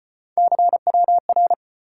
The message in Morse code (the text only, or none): CWR